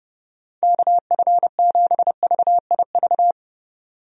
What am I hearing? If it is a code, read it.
Morse code, heard as a keyed tone, about 30 wpm, 700 Hz. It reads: KF7VIV
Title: USA Call Signs - 6 Characters - 30wpm